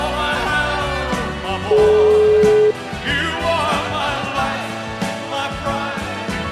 Ringback tone (Europe, 425 Hz), with noise.